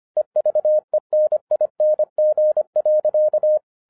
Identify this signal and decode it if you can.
Morse code, heard as a keyed tone, about 25 wpm, 610 Hz. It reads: EVENING.